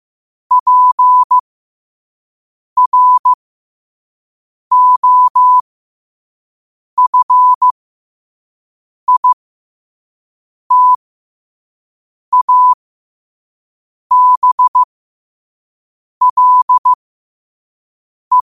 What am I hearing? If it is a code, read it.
Morse code, heard as a keyed tone, about 15 wpm, spaced as for 5 wpm, 1000 Hz. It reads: PROFITABLE
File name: qotdfiles / 5wpm